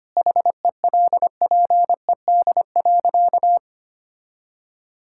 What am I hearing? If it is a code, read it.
Morse code, heard as a keyed tone, about 25 wpm, 700 Hz. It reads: HELPED.
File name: practice / 25wpm / Sentences with 4-Word Phrases - 25wpm